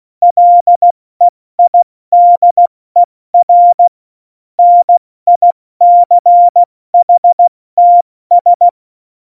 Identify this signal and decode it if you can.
Morse code, heard as a keyed tone, about 16 wpm, 700 Hz. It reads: LEIDER NICHTS
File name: Morsekurs / Witze / Witz 27. Oktober 2020, 70 F80